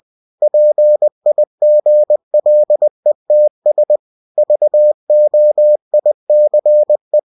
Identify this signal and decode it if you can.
Morse code, heard as a keyed tone, about 20 wpm, 600 Hz. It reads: PIGLETS VOICE
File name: cw-books / winnie-the-pooh / winnie-20wpm-page-0072